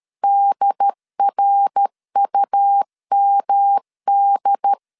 Morse code, heard as a keyed tone, about 13 wpm, 800 Hz. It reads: DRUMD
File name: CW / DET / 1 3 1500C S C 12wpm 2750